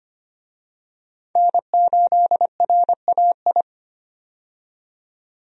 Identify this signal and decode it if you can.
Morse code, heard as a keyed tone, about 25 wpm, 690 Hz. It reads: N8RAI
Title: USA Call Signs - 5 Characters - 25wpm